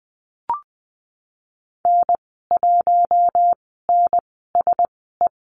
Morse code, heard as a keyed tone, about 20 wpm, spaced as for 15 wpm, 700 Hz. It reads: N1NSE
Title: 042 Lesson 8g - Number 1 - Call Signs 20wpm